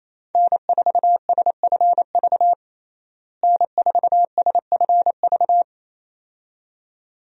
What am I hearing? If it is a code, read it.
Morse code, heard as a keyed tone, about 28 wpm, 700 Hz. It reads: N4SFV N4SFV
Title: USA Call Signs - 5 Characters - Repeated 2x - 28wpm